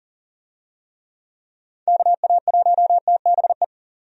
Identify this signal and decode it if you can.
Morse code, heard as a keyed tone, about 40 wpm, 690 Hz. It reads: KA1TBE
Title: USA Call Signs - 6 Characters - 40wpm